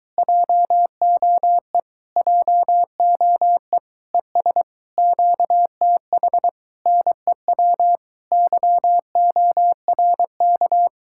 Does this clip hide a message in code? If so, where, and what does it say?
Yes: Morse code from 0.2 to 10.9 s, heard as a keyed tone, about 23 wpm, 700 Hz. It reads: JOE JOE ES QTH NEW YORK